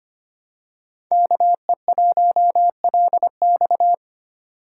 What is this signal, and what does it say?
Morse code, heard as a keyed tone, about 25 wpm, 700 Hz. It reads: KE1LX
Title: USA Call Signs - 3x Character Spacing - 25wpm